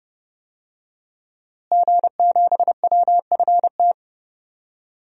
Morse code, heard as a keyed tone, about 31 wpm, 700 Hz. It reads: G7WFT
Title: DXCC-100 Call Signs - 5 Characters - Repeated 2x - 30wpm